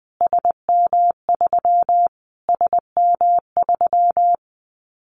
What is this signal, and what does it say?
Morse code, heard as a keyed tone, about 20 wpm, 700 Hz. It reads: SM3 SM3
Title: Callsign Prefixes for Top 25 DXCC - Repeated 3x - 20wpm